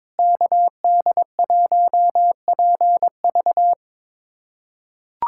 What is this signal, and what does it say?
Morse code, heard as a keyed tone, about 21 wpm, 700 Hz. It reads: KD1PV
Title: USA Callsign DE Callsign 22wpm